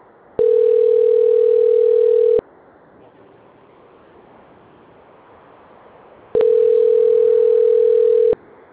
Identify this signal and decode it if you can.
Ringback tone (North America), 440 + 480 Hz.